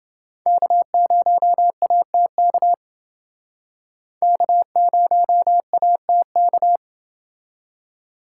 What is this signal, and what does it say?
Morse code, heard as a keyed tone, about 30 wpm, 700 Hz. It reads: K0ATK K0ATK